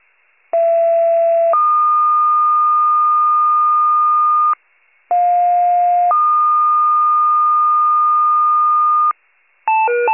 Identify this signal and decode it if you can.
Two-tone page, heard as a one-second tone, then a three-second tone, about 670 then 1220 Hz, about 710 then 1220 Hz. It is heard from 0.5 to 9.1 s.
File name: wcalert